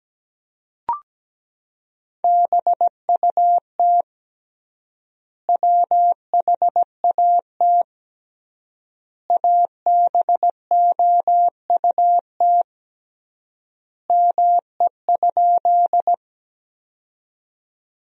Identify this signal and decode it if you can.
Morse code, heard as a keyed tone, about 17 wpm, 700 Hz. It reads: BUT WHAT ABOUT ME?
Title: 3x Word-Spacing - Sentences from Top 200 Words - 17wpm